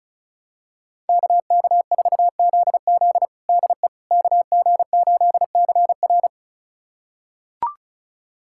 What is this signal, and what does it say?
Morse code, heard as a keyed tone, about 36 wpm, 700 Hz. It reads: KK4ZZ DE KG8CR
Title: USA Callsign DE Callsign - 5 Characters - 35wpm